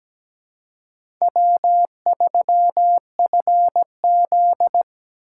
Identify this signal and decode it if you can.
Morse code, heard as a keyed tone, about 17 wpm, 700 Hz. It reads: W3FZ
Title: USA Call Signs - 4 Characters - 4x Character Spacing - 17wpm